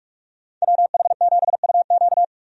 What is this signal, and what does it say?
Morse code, heard as a keyed tone, about 46 wpm, 700 Hz. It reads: WH7UQ
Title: USA Call Signs - 5 Characters - 45wpm